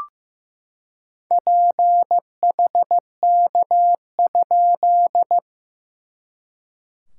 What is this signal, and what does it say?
Morse code, heard as a keyed tone, about 15 wpm, 690 Hz. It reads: PHK?
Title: SOTA QSO Practice 15wpm